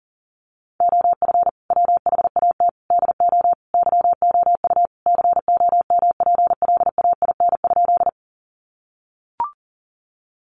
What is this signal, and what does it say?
Morse code, heard as a keyed tone, about 40 wpm, 700 Hz. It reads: OF WHAT DO YOU COMPLAIN?
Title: Sentences from Top 1300 Words 40wpm